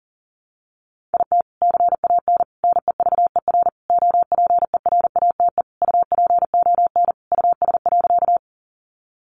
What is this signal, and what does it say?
Morse code, heard as a keyed tone, about 40 wpm, 700 Hz. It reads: IT CAN NEVER OPERATE UPON US.
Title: Sentences from Top 1500 Words 40wpm